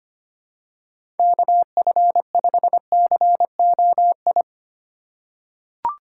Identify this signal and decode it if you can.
Morse code, heard as a keyed tone, about 25 wpm, 700 Hz. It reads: KF5COI
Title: USA Call Signs - 6 Characters - 3x Character Spacing - 25wpm